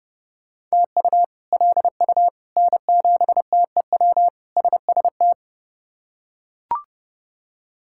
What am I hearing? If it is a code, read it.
Morse code, heard as a keyed tone, about 32 wpm, 700 Hz. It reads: TU LU N7TEW SST
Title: K1USN SST QSO Practice - 30wpm